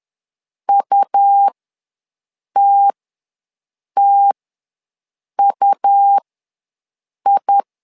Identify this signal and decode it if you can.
Morse code, heard as a keyed tone, about 11 wpm, spaced as for 6 wpm, 800 Hz. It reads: UTTUI
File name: LEZIONI / 2 UHDN / 2 U ws 10wpm 2017